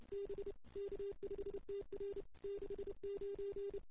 Morse code, heard as a keyed tone, about 28 wpm, 410 Hz. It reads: B K5TR 69
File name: CW / 2008-ss-cw-k5tr-op-n5rz -17:00